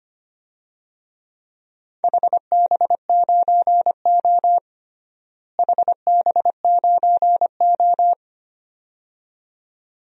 Morse code, heard as a keyed tone, about 25 wpm, 700 Hz. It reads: HB9O HB9O
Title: DXCC-100 Call Signs - 4 Characters - Repeated 2x - 25wpm